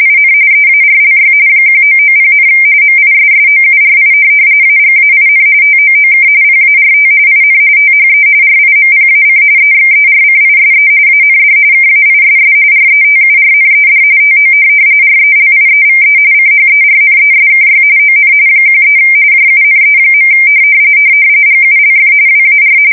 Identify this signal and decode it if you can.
RTTY, about 45.45 baud. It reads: IN EARLY 1990, MISSOURI AGAIN TOOK PART IN THE RIMPAC EXERCISE AND KAISS RETURNED TO RELIEVE CHERNESKY ON 13 JUNE. GULF WAR
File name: fsk170 / USS-Missouri